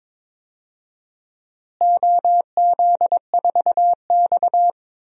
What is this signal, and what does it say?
Morse code, heard as a keyed tone, about 22 wpm, 700 Hz. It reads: OZ4X
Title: DXCC-100 Call Signs - 4 Characters - Repeated 3x - 22wpm